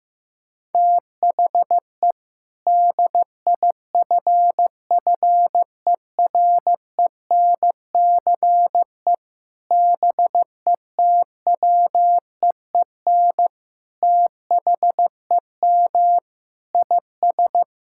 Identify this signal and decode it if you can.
Morse code, heard as a keyed tone, about 15 wpm, 700 Hz. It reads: THE DIFFERENCE BETWEEN THEM IS
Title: Sentences from Top 1600 Words 15wpm